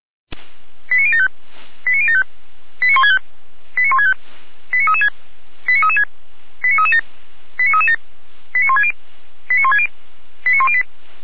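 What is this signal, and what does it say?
Five-tone selective call (ZVEI), heard as quick runs of five tones, tones of about 70 ms. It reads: 89075 89075 89165 89165 89336 89336 89337 89337 89160 89160 89108
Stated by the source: ZVEI five-tone selective call received via SDR.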